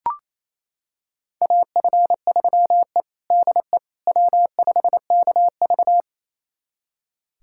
Morse code, heard as a keyed tone, about 28 wpm, 700 Hz. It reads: AF3E DE W5KV
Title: USA Callsign DE Callsign - 4 Characters - 28wpm